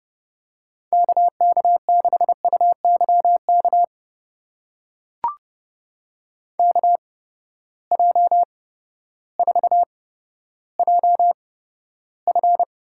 Morse code, heard as a keyed tone, about 31 wpm, spaced as for 14 wpm, 700 Hz. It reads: KK6UYK KJ4JF
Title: USA Call Signs - 6 Characters - 8x Character Spacing - 30wpm